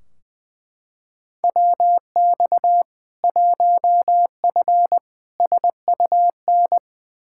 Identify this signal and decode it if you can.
Morse code, heard as a keyed tone, about 20 wpm, 700 Hz. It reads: WX 1F SUN
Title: Practice QSO - Exchange 2 - Info - 20wpm